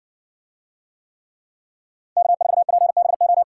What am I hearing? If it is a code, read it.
Morse code, heard as a keyed tone, about 64 wpm, 700 Hz. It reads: K4PBO